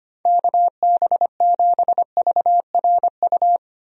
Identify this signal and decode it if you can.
Morse code, heard as a keyed tone, about 25 wpm, 700 Hz. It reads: KB7VRU
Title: USA Call Signs - 6 Characters - Repeated 2x - 25wpm